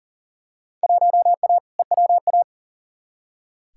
Morse code, heard as a keyed tone, about 40 wpm, 700 Hz. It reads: 1A EWA
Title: Field Day Exchanges Encore - 40wpm